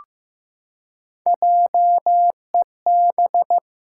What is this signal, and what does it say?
Morse code, heard as a keyed tone, about 16 wpm, 700 Hz. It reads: JEB
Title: Pangram Challenge v2 - 15wpm